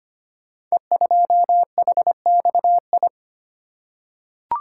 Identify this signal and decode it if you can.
Morse code, heard as a keyed tone, about 25 wpm, 690 Hz. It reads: E2HXI
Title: Code Groups - 5 Letter:Numbers - 25wpm